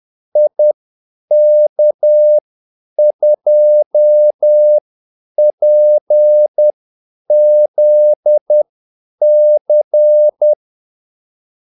Morse code, heard as a keyed tone, about 10 wpm, spaced as for 8 wpm, 600 Hz. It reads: IK2PZC
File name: Callsigns-10-8wpm